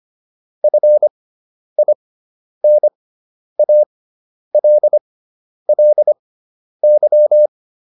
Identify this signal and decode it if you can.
Morse code, heard as a keyed tone, about 25 wpm, spaced as for 10 wpm, 600 Hz. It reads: FINALLY